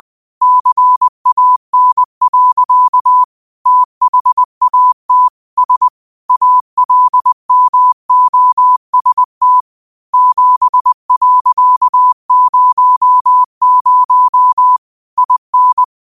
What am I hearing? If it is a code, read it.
Morse code, heard as a keyed tone, about 20 wpm, 1000 Hz. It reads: CAN. THAT S ALMOST 7.00 IN
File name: qotdfiles / 20wpm